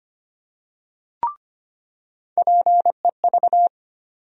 Morse code, heard as a keyed tone, about 25 wpm, 700 Hz. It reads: PEV
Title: USA Call Sign Postfixes - 3 Letters 25wpm